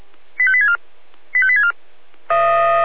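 Five-tone selective call (ZVEI), heard as quick runs of five tones, tones of about 70 ms. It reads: 85864 85864
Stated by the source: SDR-received ZVEI five-tone selective call.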